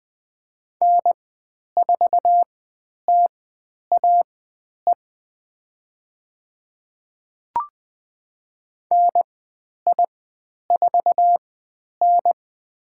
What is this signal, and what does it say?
Morse code, heard as a keyed tone, about 20 wpm, spaced as for 10 wpm, 700 Hz. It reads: N4TAE NI4N
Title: 095 Lesson 16g - Letter c - Call Signs 20wpm